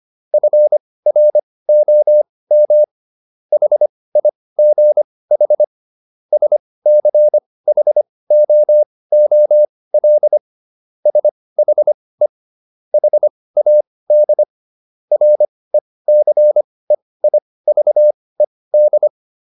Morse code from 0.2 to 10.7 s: FROM HIGH SCHOOL

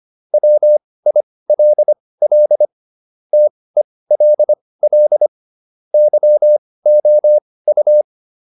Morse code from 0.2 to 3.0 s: WILL

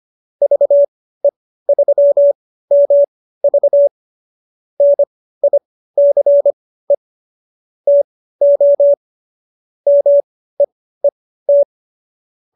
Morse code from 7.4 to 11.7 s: TO MEET

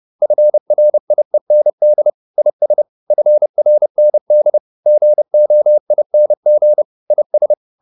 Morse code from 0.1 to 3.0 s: FRIEND IS

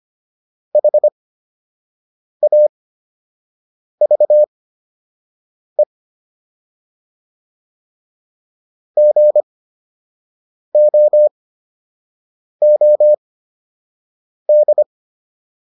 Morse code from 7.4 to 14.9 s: GOOD

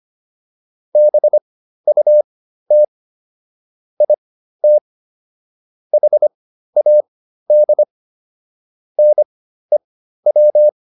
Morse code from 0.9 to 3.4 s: BUT

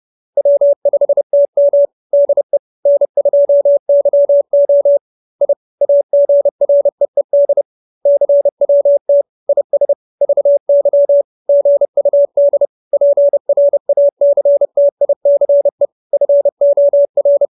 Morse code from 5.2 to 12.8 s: I AGREED CWT IS VY GUD

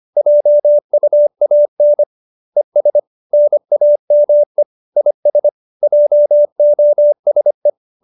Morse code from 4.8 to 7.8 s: IS JOSE

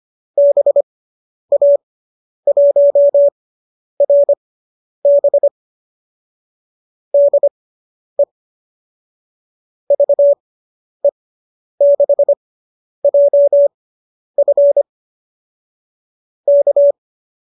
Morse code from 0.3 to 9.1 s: BA1RB DE

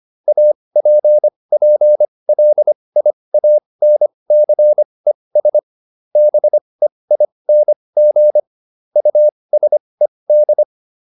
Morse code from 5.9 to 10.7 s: BEING USED

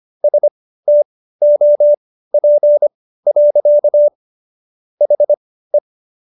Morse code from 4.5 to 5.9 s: HE

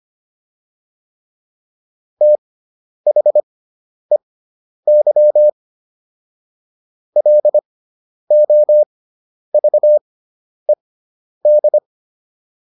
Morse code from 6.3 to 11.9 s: LOVED

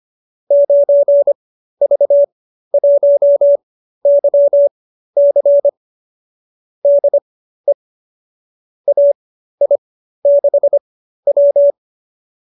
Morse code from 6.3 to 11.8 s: DE AI6W

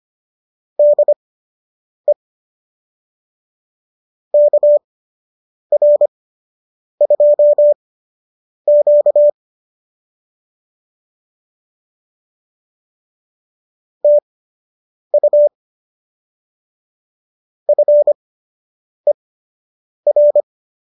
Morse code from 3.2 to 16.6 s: KR2Q TU

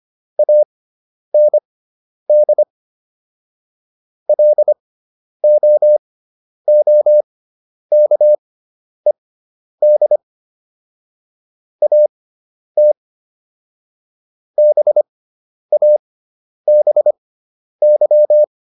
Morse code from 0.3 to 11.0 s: AND LOOKED

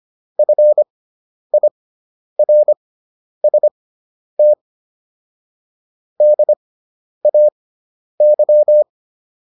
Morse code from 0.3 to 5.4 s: FIRST

